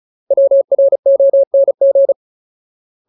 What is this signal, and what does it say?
Morse code, heard as a keyed tone, about 34 wpm, 550 Hz. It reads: WRONG
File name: Ultimate Word List / 35WPM Ultimate